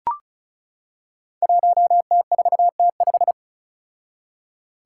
Morse code, heard as a keyed tone, about 36 wpm, 700 Hz. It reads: 1T4T5
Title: Five Challenging Letter-Number Combinations 35wpm